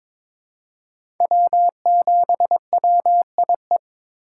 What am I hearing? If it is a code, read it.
Morse code, heard as a keyed tone, about 22 wpm, 700 Hz. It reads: W7WIE
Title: USA Call Signs - 5 Characters - 4x Character Spacing - 22wpm